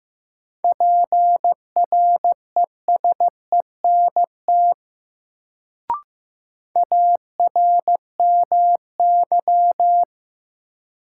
Morse code, heard as a keyed tone, about 15 wpm, 700 Hz. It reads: PRESENT ARMY
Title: Sentences from Top 800 Words - New Word Drill - 15wpm